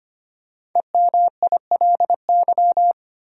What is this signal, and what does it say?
Morse code, heard as a keyed tone, about 25 wpm, 700 Hz. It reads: EMILY
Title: USA Common Names - Max 5 Letters - 25wpm